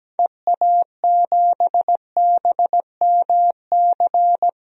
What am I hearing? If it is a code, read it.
Morse code, heard as a keyed tone, about 17 wpm, 700 Hz. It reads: EA7BMC
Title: DXCC-100 Call Signs - 6 Characters - Repeated 2x - 17wpm